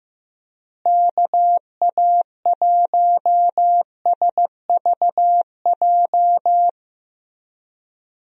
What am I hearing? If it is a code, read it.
Morse code, heard as a keyed tone, about 15 wpm, 700 Hz. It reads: KA1SVJ
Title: USA Call Signs - 6 Characters - Repeated 2x - 15wpm